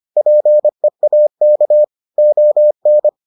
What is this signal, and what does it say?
Morse code, heard as a keyed tone, about 25 wpm, 600 Hz. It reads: PEAK ON